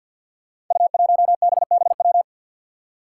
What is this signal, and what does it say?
Morse code, heard as a keyed tone, about 47 wpm, 700 Hz. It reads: U1BBW
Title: Five Challenging Letter-Number Combinations 50wpm